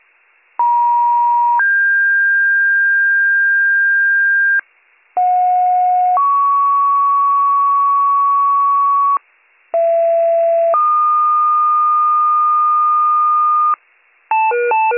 Two-tone page, heard as a one-second tone, then a three-second tone, about 950 then 1600 Hz, about 730 then 1120 Hz, about 670 then 1220 Hz. It is heard from 0.6 to 13.7 s.